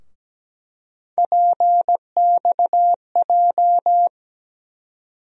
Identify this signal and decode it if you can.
Morse code, heard as a keyed tone, about 17 wpm, 700 Hz. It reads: PXJ
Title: USA Call Sign Postfixes - 3 Letters 17wpm